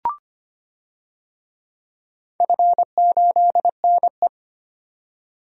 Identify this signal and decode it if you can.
Morse code, heard as a keyed tone, about 25 wpm, 700 Hz. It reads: F8NE
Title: DXCC-100 Call Signs - 4 Characters - 25wpm